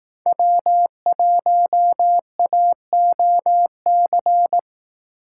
Morse code, heard as a keyed tone, about 18 wpm, 700 Hz. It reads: W1AOC
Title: Speed Racing - USA 1x3 Call Signs 15wpm